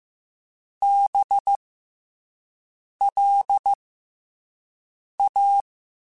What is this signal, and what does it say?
Morse code, heard as a keyed tone, about 15 wpm, spaced as for 5 wpm, 800 Hz. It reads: BLA
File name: Special INGs-mp3s / Special INGs-mp3s / SlowToFaster ing @10